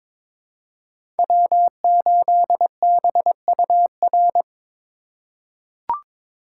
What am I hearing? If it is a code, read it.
Morse code, heard as a keyed tone, about 22 wpm, 700 Hz. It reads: W8BUR
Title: USA Call Signs - 5 Characters - 2x Character Spacing - 22wpm